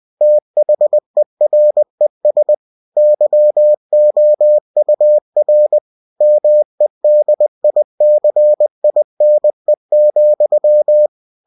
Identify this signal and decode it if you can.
Morse code, heard as a keyed tone, about 20 wpm, 600 Hz. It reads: THERES YOUR MEDICINE,
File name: cw-books / winnie-the-pooh / winnie-20wpm-page-0073